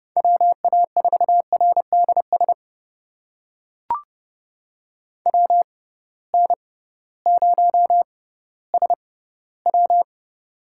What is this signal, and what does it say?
Morse code, heard as a keyed tone, about 32 wpm, spaced as for 17 wpm, 700 Hz. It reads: WA4RDS WN0SW
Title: USA Call Signs - 6x Character Spacing - 30wpm